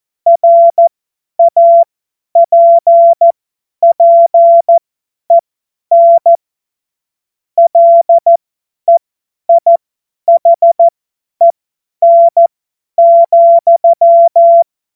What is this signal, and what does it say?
Morse code, heard as a keyed tone, about 14 wpm, spaced as for 10 wpm, 680 Hz. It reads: RAPPEN LEIHEN,
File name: Witz-23-September-2025-50-F70